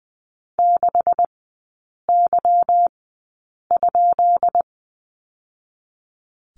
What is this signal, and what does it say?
Morse code, heard as a keyed tone, about 20 wpm, spaced as for 8 wpm, 700 Hz. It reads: 6Y?
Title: 152 Lesson 25e - Letter p - Three Characters 20wpm